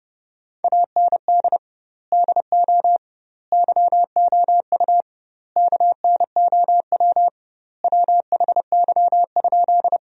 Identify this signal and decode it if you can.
Morse code, heard as a keyed tone, about 32 wpm, 700 Hz. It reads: AND DO YOU KNOW WHY?
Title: 2x Word-Spacing - Sentences from Top 200 Words - 30wpm